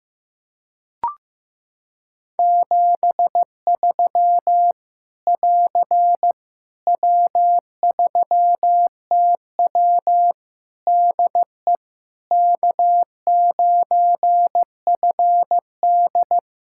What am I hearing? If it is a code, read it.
Morse code, heard as a keyed tone, about 15 wpm, 700 Hz. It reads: 73 <AR> W3TW DE K9FD
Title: Practice QSO - Exchange 4 - Ending - 15wpm